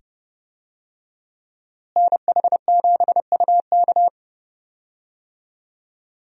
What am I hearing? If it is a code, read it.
Morse code, heard as a keyed tone, about 29 wpm, 700 Hz. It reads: NH7UK
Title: USA Call Signs - 5 Characters - Repeated 2x - 30wpm